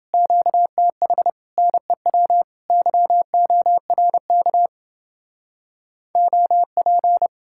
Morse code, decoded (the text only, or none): QTH NEW YORK OP